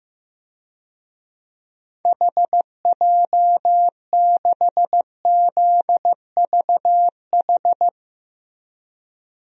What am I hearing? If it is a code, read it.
Morse code, heard as a keyed tone, about 15 wpm, 700 Hz. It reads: HJ6ZVH
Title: DXCC-100 Call Signs - 6 Characters - Repeated 3x - 15wpm